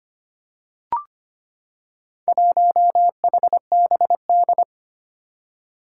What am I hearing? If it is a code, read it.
Morse code, heard as a keyed tone, about 25 wpm, 700 Hz. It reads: 1HBD
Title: Four Challenging Letter-Number Combinations 25wpm